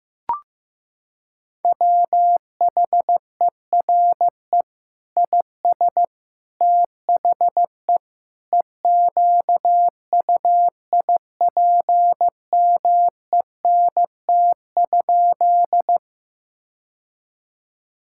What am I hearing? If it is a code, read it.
Morse code, heard as a keyed tone, about 16 wpm, 700 Hz. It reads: WHERE IS THE EQUIPMENT?
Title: Sentences from Top 1000 Words - Review Part 5 - 15wpm